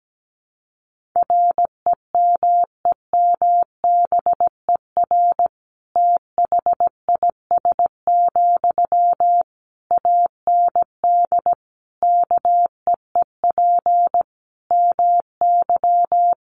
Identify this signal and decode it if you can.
Morse code, heard as a keyed tone, about 17 wpm, 700 Hz. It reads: REMEMBER THIS, AND KEEP MY
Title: Sentences from Top 1500 Words 17wpm